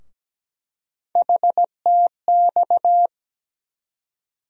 Morse code, heard as a keyed tone, about 17 wpm, 700 Hz. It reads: HTX
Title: USA Call Sign Postfixes - 3 Letters 17wpm